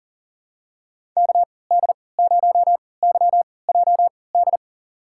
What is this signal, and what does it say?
Morse code, heard as a keyed tone, about 39 wpm, spaced as for 23 wpm, 700 Hz. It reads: KD0YJD
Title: USA Call Signs - 6 Characters - 3x Character Spacing - 40wpm